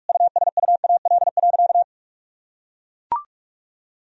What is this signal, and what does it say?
Morse code, heard as a keyed tone, about 44 wpm, 700 Hz. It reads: USUAL.
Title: Sentences from Top 1000 Words - Review Part 2 - 45wpm